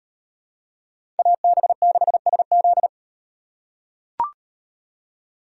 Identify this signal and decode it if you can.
Morse code, heard as a keyed tone, about 38 wpm, 700 Hz. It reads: AB6SZ